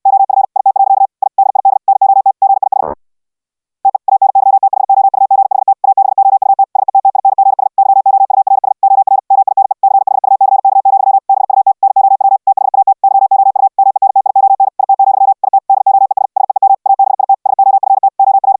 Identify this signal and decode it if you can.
Morse code, heard as a keyed tone, about 69 wpm, 780 Hz. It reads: OF IT. I DID NOT GIS TE ANYTHOUGHT ABOUT SETTINGS YOURS OR MINE CHUCK, BUT NOW THAT YOU MENTION IT, IT DOES SEEM ABIT SOFT OR